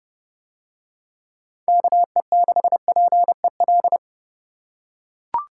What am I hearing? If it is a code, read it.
Morse code, heard as a keyed tone, about 28 wpm, 700 Hz. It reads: KE6PEL